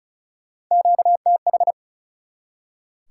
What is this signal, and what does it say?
Morse code, heard as a keyed tone, about 48 wpm, spaced as for 42 wpm, 700 Hz. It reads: QTH
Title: Speed-Racing Top 180 QSO Elements 35wpm